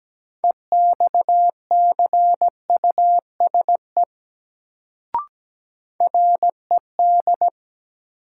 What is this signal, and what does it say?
Morse code, heard as a keyed tone, about 17 wpm, 700 Hz. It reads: EXCUSE RED
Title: Sentences from Top 700 Words - New Word Drill - 17wpm